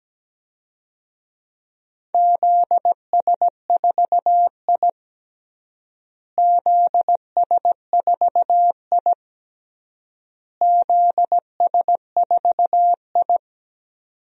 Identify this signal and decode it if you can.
Morse code, heard as a keyed tone, about 17 wpm, 700 Hz. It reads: ZS4I ZS4I ZS4I
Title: DXCC-100 Call Signs - 4 Characters - Repeated 3x - 17wpm